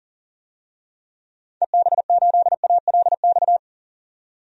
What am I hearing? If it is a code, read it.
Morse code, heard as a keyed tone, about 37 wpm, 700 Hz. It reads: EB8ALX
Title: DXCC-100 Call Signs - 6 Characters - 40wpm